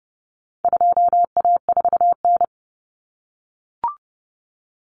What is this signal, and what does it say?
Morse code, heard as a keyed tone, about 30 wpm, 700 Hz. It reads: 2A4N
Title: Common Cut Numbers - 4-Digit 30wpm